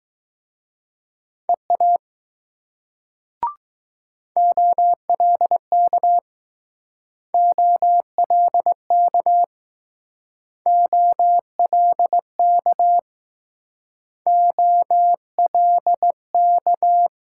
Morse code, heard as a keyed tone, about 20 wpm, 700 Hz. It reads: EA OLK OLK OLK OLK